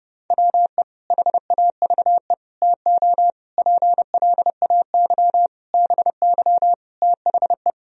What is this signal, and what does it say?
Morse code, heard as a keyed tone, about 32 wpm, 700 Hz. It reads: WE HAVE TO PLAY BY THE
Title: Sentences from Top 1000 Words - Review Part 3 - 30wpm